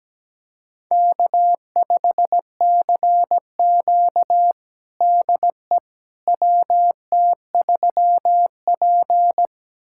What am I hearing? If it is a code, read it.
Morse code, heard as a keyed tone, about 17 wpm, 700 Hz. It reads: K5CQ DE WT3P
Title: USA Callsign DE Callsign - 4 Characters - 17wpm